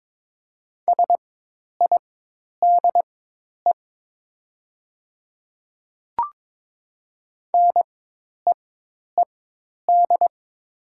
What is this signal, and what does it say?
Morse code, heard as a keyed tone, about 22 wpm, spaced as for 10 wpm, 700 Hz. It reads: SIDE NEED
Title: 4x Character Spacing - Top 300 Words - 22wpm